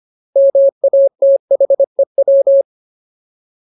Morse code, heard as a keyed tone, about 25 wpm, 550 Hz. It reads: MATHEW
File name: Ultimate Word List / 25WPM Ultimate